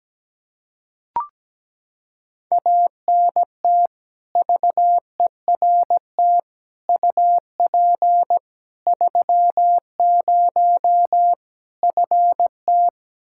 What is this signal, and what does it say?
Morse code, heard as a keyed tone, about 17 wpm, 700 Hz. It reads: ANT VERT UP 30 FT
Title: K1USN SST QSO Practice 17wpm